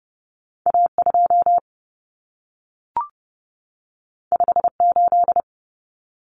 Morse code, heard as a keyed tone, about 30 wpm, 700 Hz. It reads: A2 58
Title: Common Cut Numbers - 2-Digit 30wpm